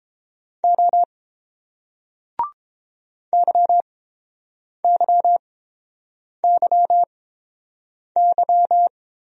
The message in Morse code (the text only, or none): O YYYY